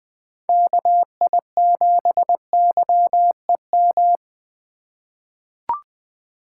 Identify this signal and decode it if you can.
Morse code, heard as a keyed tone, about 20 wpm, 700 Hz. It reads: KI7YEM
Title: USA Call Signs - 8x Character Spacing - 20wpm